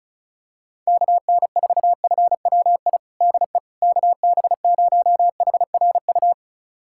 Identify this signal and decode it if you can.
Morse code, heard as a keyed tone, about 36 wpm, 700 Hz. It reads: KN4FWI DE KB0HRU
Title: USA Callsign DE Callsign - 6 Characters - 35wpm